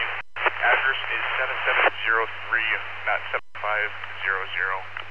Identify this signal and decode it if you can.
Emergency Alert System (SAME) tones, heard as harsh buzzing bursts, with speech. It is heard from 0.6 to 4.7 s.